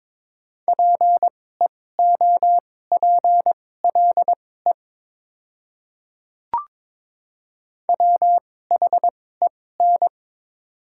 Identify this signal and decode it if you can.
Morse code, heard as a keyed tone, about 22 wpm, spaced as for 16 wpm, 700 Hz. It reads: PEOPLE WHEN
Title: 2x Character Spacing - Top 100 Words - 22wpm